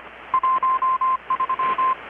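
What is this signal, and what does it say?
Morse code, heard as a keyed tone, about 25 wpm, 1040 Hz, with radio noise. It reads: 13
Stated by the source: Morse from KPH's Night of Nights XX maritime radiotelegraphy event, received in Chiapas, Mexico about 3,800 km away on a Kenwood R-600 with a grounded 10 m long-wire antenna.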